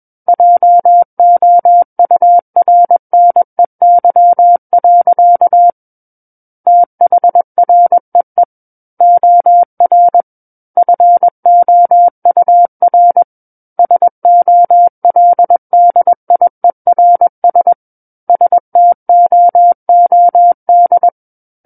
Morse code, heard as a keyed tone, about 21 wpm, 700 Hz. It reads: JOURNEY. THREE OR FOUR SOLDIERS STOOD
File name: Book One / 12-Chapter 12 - 21 WPM